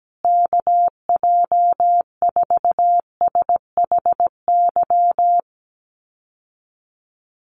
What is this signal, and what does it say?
Morse code, heard as a keyed tone, about 17 wpm, 700 Hz. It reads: KJ4SHY
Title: USA Call Signs Repeated 3x 17wpm